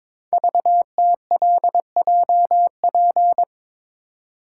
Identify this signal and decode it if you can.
Morse code, heard as a keyed tone, about 22 wpm, 700 Hz. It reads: VTLJP
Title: Code Groups - 5 Letters - 22wpm